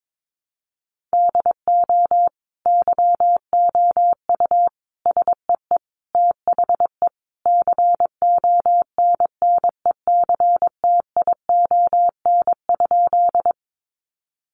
Morse code, heard as a keyed tone, about 22 wpm, 700 Hz. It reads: DO YOU SEE THE CONNECTION?
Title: Sentences from Top 1300 Words 22wpm